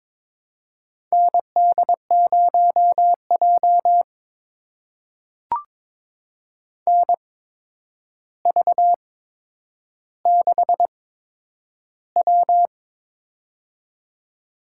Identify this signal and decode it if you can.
Morse code, heard as a keyed tone, about 22 wpm, spaced as for 9 wpm, 700 Hz. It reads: ND0J NV6W
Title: USA Call Signs - 4 Characters - 8x Character Spacing - 22wpm